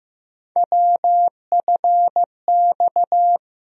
Morse code, heard as a keyed tone, about 15 wpm, 700 Hz. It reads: WFX
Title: USA Call Sign Postfixes - 3 Letters 15wpm